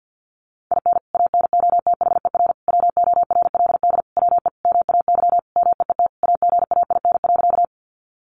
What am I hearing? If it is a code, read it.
Morse code, heard as a keyed tone, about 49 wpm, 700 Hz. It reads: IN ANOTHER WORLD WE MAY MEET AGAIN.
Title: Sentences from Top 1100 Words 50wpm